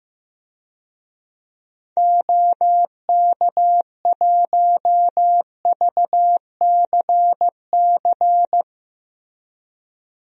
Morse code, heard as a keyed tone, about 15 wpm, 700 Hz. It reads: OK1VCC
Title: DXCC-100 Call Signs - 6 Characters - Repeated 2x - 15wpm